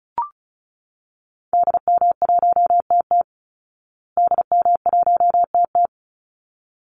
Morse code, heard as a keyed tone, about 34 wpm, 700 Hz. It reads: DM1TT DM1TT
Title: Call Signs Repeated 3x - 35wpm